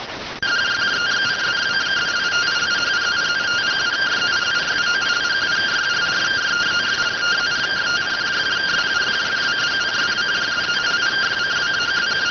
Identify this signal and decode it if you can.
Two-tone teleprinter signal, about 170 Hz shift.